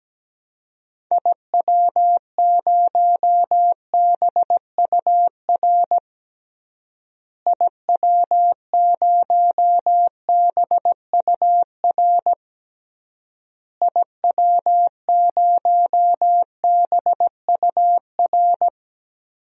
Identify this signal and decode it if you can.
Morse code, heard as a keyed tone, about 17 wpm, 700 Hz. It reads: IW0BUR IW0BUR IW0BUR
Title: DXCC-100 Call Signs - 6 Characters - Repeated 3x - 17wpm